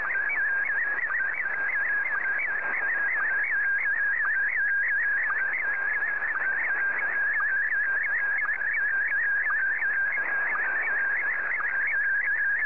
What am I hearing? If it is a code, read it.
SSTV (slow-scan television), Scottie DX.
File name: sstv / SDX